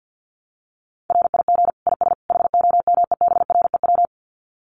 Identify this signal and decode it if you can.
Morse code, heard as a keyed tone, about 50 wpm, 700 Hz. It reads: RIG IS HOMEBREW
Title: Common QSO Phrases 50wpm